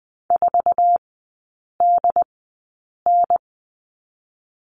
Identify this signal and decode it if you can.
Morse code, heard as a keyed tone, about 20 wpm, spaced as for 8 wpm, 700 Hz. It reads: 4DN
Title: 074 Lesson 13e - Letter L - Three Characters 20wpm